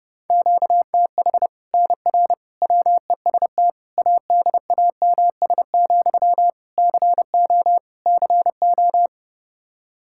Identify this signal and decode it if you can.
Morse code, heard as a keyed tone, about 30 wpm, 700 Hz. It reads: QTH NR WEST ADAMS, CO CO
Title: Practice QSO - Exchange 1 - Beginning - 30wpm